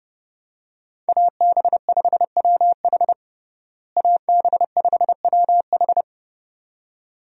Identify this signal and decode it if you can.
Morse code, heard as a keyed tone, about 30 wpm, 700 Hz. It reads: AB5WH AB5WH
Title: USA Call Signs - 5 Characters - Repeated 2x - 30wpm